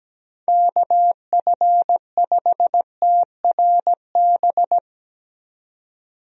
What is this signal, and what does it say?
Morse code, heard as a keyed tone, about 17 wpm, 700 Hz. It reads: KF5TRB
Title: USA Call Signs - 6 Characters - 17wpm